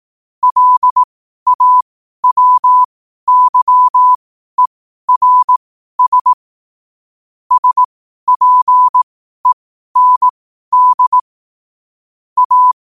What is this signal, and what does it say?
Morse code, heard as a keyed tone, about 18 wpm, spaced as for 12 wpm, 1000 Hz. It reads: LAWYERS SPEND A
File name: qotdfiles / 12wpm